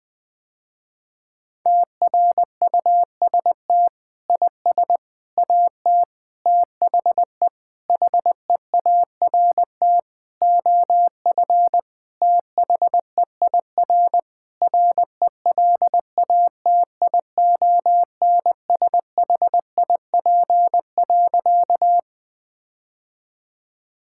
Morse code, heard as a keyed tone, about 20 wpm, 700 Hz. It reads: TRUST IS AT THE HEART OF THEIR RELATIONSHIP.
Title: Sentences with 5-Word Phrases - Spoken Beginning - 20wpm